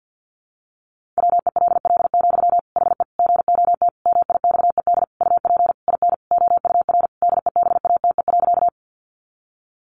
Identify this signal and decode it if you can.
Morse code, heard as a keyed tone, about 50 wpm, 700 Hz. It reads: WELL, HE GOT MIXED UP IN OUR DEBATE.
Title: Sentences from Top 1500 Words 50wpm